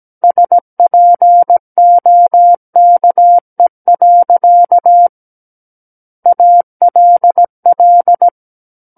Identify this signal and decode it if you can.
Morse code, heard as a keyed tone, about 17 wpm, 700 Hz. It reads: SPOKE. ALL
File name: Book One / 08-Chapter 8 - 17 WPM